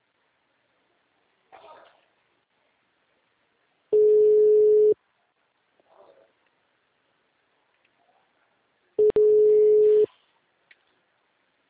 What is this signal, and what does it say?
Ringback tone (Europe, 425 Hz).
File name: out- -2025